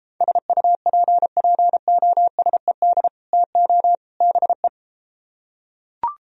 Morse code, heard as a keyed tone, about 33 wpm, 700 Hz. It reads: SUPPOSED TO BE